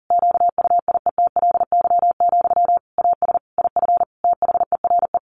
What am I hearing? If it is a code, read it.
Morse code, heard as a keyed tone, about 40 wpm, 700 Hz. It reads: QUIETLY, AS IF THERE
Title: A Sound Of Thunder 40wpm